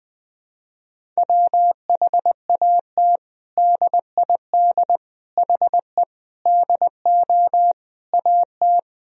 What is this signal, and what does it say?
Morse code, heard as a keyed tone, about 21 wpm, 700 Hz. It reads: WHAT DID HE DO AT
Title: Sentences from Top 1000 Words - Review Part 1 - 20wpm